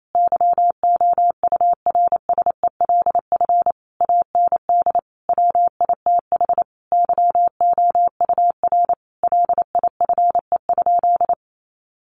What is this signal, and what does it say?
Morse code, heard as a keyed tone, about 28 wpm, 700 Hz. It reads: YOURSELF AND WITH YOUR LIFE?
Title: Sentences from Top 900 Words 28wpm